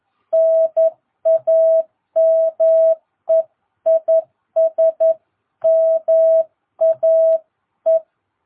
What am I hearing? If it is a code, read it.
Morse code, heard as a keyed tone, about 11 wpm, 660 Hz. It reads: NAME IS MAE
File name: jh8cbh / 2022cw